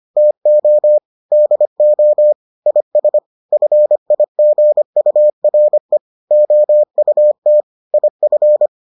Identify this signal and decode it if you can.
Morse code, heard as a keyed tone, about 25 wpm, 600 Hz. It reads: TO DO IS FIGURE OUT IF